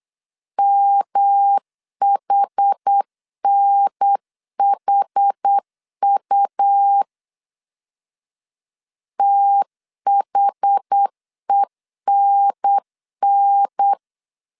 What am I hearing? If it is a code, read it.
Morse code, heard as a keyed tone, about 8 wpm, 800 Hz. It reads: MHNHU THENN
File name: CW / DET / 1 3 1500C S B 8wpm 4141